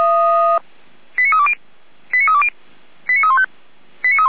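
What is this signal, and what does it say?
Five-tone selective call (ZVEI), heard as quick runs of five tones, tones of about 80 ms. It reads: 89320 89320 89315
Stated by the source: ZVEI five-tone call 89280 received on an SDR.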